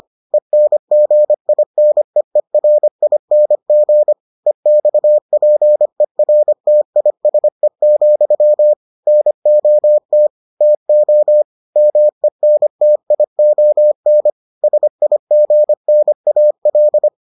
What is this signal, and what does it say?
Morse code, heard as a keyed tone, about 25 wpm, 600 Hz. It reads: ENGINEERING EXPERTISE, NOT TO MENTION SIGNAL